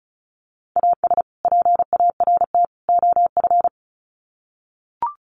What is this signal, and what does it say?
Morse code, heard as a keyed tone, about 34 wpm, 700 Hz. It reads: AS PART OF
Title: Sets of 3 Words 35wpm